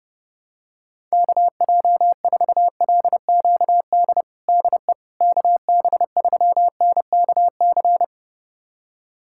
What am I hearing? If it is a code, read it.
Morse code, heard as a keyed tone, about 31 wpm, 700 Hz. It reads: KJ4LQD DE KB3NKC
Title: USA Callsign DE Callsign - 6 Characters - 30wpm